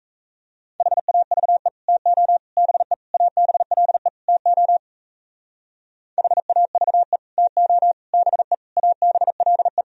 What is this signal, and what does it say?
Morse code, heard as a keyed tone, about 32 wpm, 700 Hz. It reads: HAVE TO BE ABLE TO HAVE TO BE ABLE
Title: Speed-Racing - Sets of 5 Words 28wpm